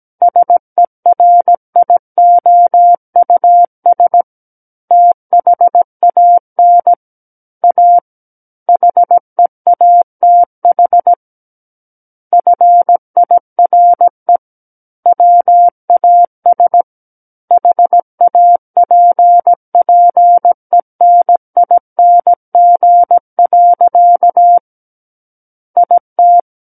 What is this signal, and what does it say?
Morse code, heard as a keyed tone, about 17 wpm, 700 Hz. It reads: SERIOUS THAN A HEATH FIRE WAS HAPPENING. IT
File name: Book One / 08-Chapter 8 - 17 WPM